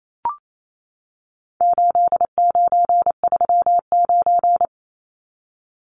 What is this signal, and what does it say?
Morse code, heard as a keyed tone, about 28 wpm, 700 Hz. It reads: 8939
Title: Numbers - 4-Digit 28wpm